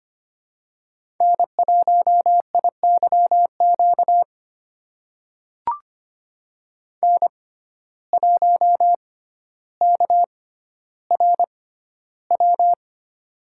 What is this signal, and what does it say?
Morse code, heard as a keyed tone, about 25 wpm, spaced as for 13 wpm, 700 Hz. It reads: N1IYQ N1KRW
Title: USA Call Signs - 6x Character Spacing - 25wpm